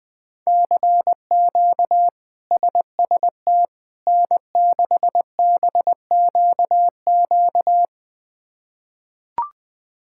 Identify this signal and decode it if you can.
Morse code, heard as a keyed tone, about 21 wpm, 700 Hz. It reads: CQ SST N6BQQ
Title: K1USN SST QSO Practice - 20wpm